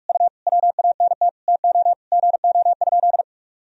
Morse code, heard as a keyed tone, about 41 wpm, 700 Hz. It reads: U WANT TO GO?
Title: Sentences from Top 300 Words and Abbreviations 45wpm